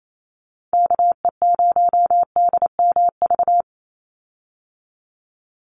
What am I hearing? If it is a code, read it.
Morse code, heard as a keyed tone, about 28 wpm, 700 Hz. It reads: KE0DMV
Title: USA Call Signs 28wpm